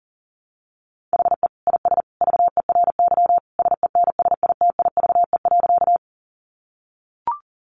Morse code, heard as a keyed tone, about 40 wpm, 700 Hz. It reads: HE IS VERY SENSITIVE.
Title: Sentences from Top 1400 Words 40wpm